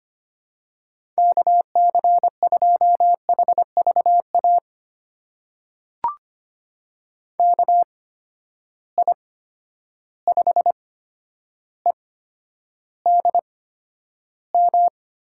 Morse code, heard as a keyed tone, about 25 wpm, spaced as for 11 wpm, 700 Hz. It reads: KC2HVA KI5EDM
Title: USA Call Signs - 6 Characters - 8x Character Spacing - 25wpm